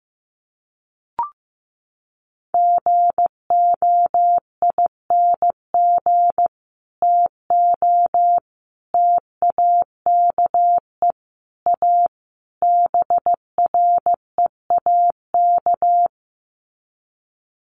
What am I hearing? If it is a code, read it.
Morse code, heard as a keyed tone, about 15 wpm, 700 Hz. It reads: GOING TO TAKE A BREAK
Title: Sets of 5 Words 15wpm